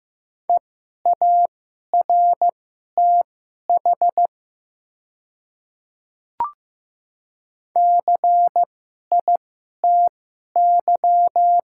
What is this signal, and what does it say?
Morse code, heard as a keyed tone, about 15 wpm, spaced as for 11 wpm, 700 Hz. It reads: EARTH CITY
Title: 2x Character Spacing - Top 300 Words - 15wpm